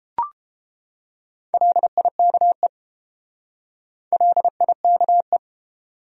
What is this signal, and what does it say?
Morse code, heard as a keyed tone, about 29 wpm, 700 Hz. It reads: LIKE LIKE